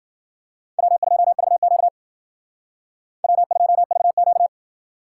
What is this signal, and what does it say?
Morse code, heard as a keyed tone, about 52 wpm, 700 Hz. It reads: W2VQ W2VQ